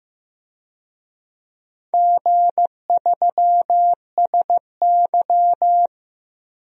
Morse code, heard as a keyed tone, about 15 wpm, 700 Hz. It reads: G3SY
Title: DXCC-100 Call Signs - 4 Characters - Repeated 3x - 15wpm